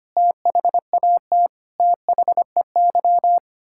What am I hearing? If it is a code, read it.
Morse code, heard as a keyed tone, about 25 wpm, 700 Hz. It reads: THAT THEY
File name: Sets of 9 Words - 25wpm